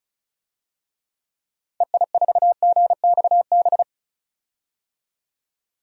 Morse code, heard as a keyed tone, about 36 wpm, 700 Hz. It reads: EI4GXB
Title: DXCC-100 Call Signs - 6 Characters - 35wpm